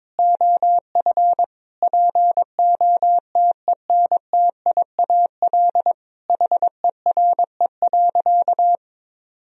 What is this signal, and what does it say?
Morse code, heard as a keyed tone, about 22 wpm, 700 Hz. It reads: OF POTENTIAL HERE.
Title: Sentences from Top 1000 Words - Review Part 4 - 22wpm